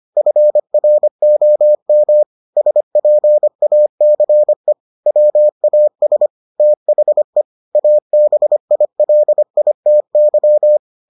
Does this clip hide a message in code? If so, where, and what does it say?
Yes: Morse code from 0.2 to 10.8 s, heard as a keyed tone, about 25 wpm, 600 Hz. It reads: FROM SPACE WAS THE ABILITY